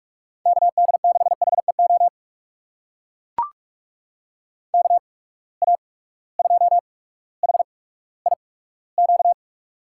Morse code, heard as a keyed tone, about 44 wpm, spaced as for 19 wpm, 700 Hz. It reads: KD6HEO KA2HIQ
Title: USA Call Signs - 6 Characters - 8x Character Spacing - 45wpm